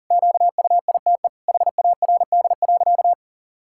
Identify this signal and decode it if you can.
Morse code, heard as a keyed tone, about 40 wpm, 700 Hz. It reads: QUITE HARD.
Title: Pangram Challenge v2 - 40wpm